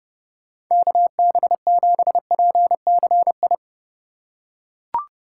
Morse code, heard as a keyed tone, about 31 wpm, 700 Hz. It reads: KB7PCI